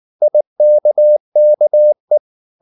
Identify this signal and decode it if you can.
Morse code, heard as a keyed tone, about 19 wpm, 600 Hz. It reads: IKKE
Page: Ferie 19 wpm